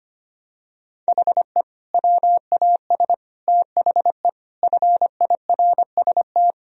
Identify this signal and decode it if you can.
Morse code, heard as a keyed tone, about 25 wpm, 700 Hz. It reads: HE WAS THE FIRST
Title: Sentences from Top 1000 Words - Review Part 1 - 25wpm